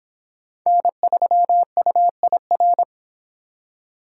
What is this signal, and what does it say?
Morse code, heard as a keyed tone, about 26 wpm, 700 Hz. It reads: N3UIR